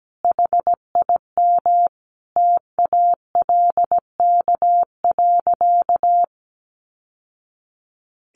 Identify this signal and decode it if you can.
Morse code, heard as a keyed tone, about 17 wpm, 700 Hz. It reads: HIM TALK.
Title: Sentences from Top 600 Words 17wpm